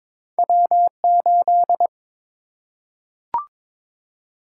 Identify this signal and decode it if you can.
Morse code, heard as a keyed tone, about 22 wpm, 700 Hz. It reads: W8
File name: practice / 22wpm / Two Letter-Number Combinations - 22wpm